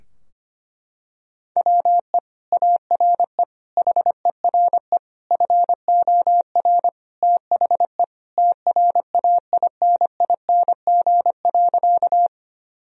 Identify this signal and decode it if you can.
Morse code, heard as a keyed tone, about 25 wpm, 700 Hz. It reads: WE ARE HERE FOR THE TRAINING.
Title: Sentences from Top 1000 Words - Review Part 4 - 25wpm